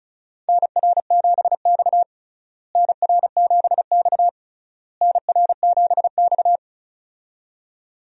Morse code, heard as a keyed tone, about 34 wpm, 700 Hz. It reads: NR7X NR7X NR7X
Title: USA Call Signs - 4 Characters - Repeated 3x - 35wpm